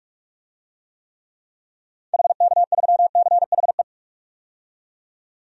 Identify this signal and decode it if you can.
Morse code, heard as a keyed tone, about 42 wpm, 700 Hz. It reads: HK3CHE
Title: DXCC-100 Call Signs - 6 Characters - 45wpm